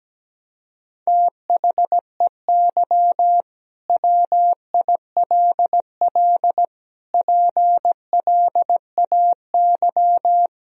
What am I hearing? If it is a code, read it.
Morse code, heard as a keyed tone, about 17 wpm, 700 Hz. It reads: THEY WILL PLAY
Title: Sentences from Top 1000 Words - Review Part 4 - 17wpm